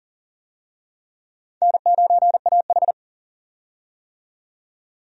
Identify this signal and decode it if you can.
Morse code, heard as a keyed tone, about 40 wpm, 700 Hz. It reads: N9AH
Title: USA Call Signs - 4 Characters - 40wpm